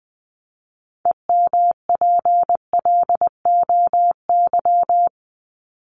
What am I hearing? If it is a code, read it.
Morse code, heard as a keyed tone, about 19 wpm, 700 Hz. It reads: EMPLOY
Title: Top 1751-2000 Words 20wpm